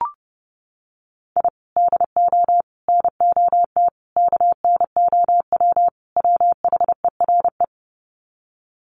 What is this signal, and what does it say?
Morse code, heard as a keyed tone, about 30 wpm, 700 Hz. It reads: I DO NOT KNOW WHERE
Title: Sets of 5 Words 30wpm